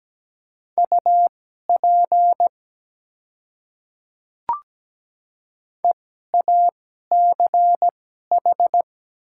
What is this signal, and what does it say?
Morse code, heard as a keyed tone, about 17 wpm, spaced as for 12 wpm, 700 Hz. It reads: UP EACH
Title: 2x Character Spacing - Top 100 Words - 17wpm